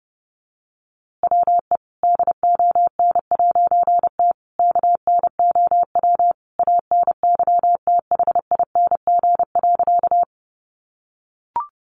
Morse code, heard as a keyed tone, about 30 wpm, 700 Hz. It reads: WE DON'T KNOW ANYTHING.
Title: A Sound Of Thunder 30wpm